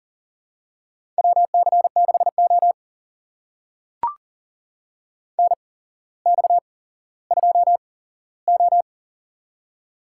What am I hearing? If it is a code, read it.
Morse code, heard as a keyed tone, about 39 wpm, spaced as for 17 wpm, 700 Hz. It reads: WC6O NX2O
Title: USA Call Signs - 4 Characters - 8x Character Spacing - 40wpm